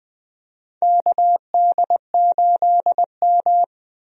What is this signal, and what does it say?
Morse code, heard as a keyed tone, about 20 wpm, 700 Hz. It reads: KD8M
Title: USA Call Signs - 4 Characters - 20wpm